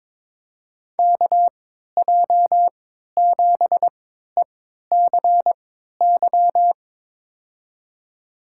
Morse code, heard as a keyed tone, about 22 wpm, spaced as for 12 wpm, 700 Hz. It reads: KJ7ECY
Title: USA Call Signs - 6 Characters - 3x Character Spacing - 22wpm